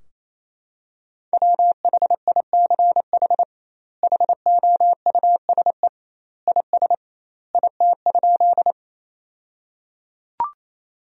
Morse code, heard as a keyed tone, about 28 wpm, 700 Hz. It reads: WHICH HOUSE IS IT?
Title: 2x Word-Spacing - Sentences from Top 200 Words - 28wpm